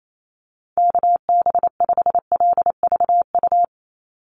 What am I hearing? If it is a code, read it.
Morse code, heard as a keyed tone, about 28 wpm, 700 Hz. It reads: KB5LVU
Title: USA Call Signs 28wpm